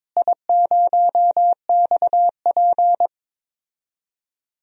Morse code, heard as a keyed tone, about 21 wpm, 700 Hz. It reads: I0XP
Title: DXCC-100 Call Signs - 4 Characters - Repeated 3x - 22wpm